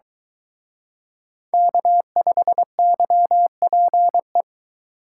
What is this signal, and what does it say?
Morse code, heard as a keyed tone, about 23 wpm, 700 Hz. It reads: K5YPE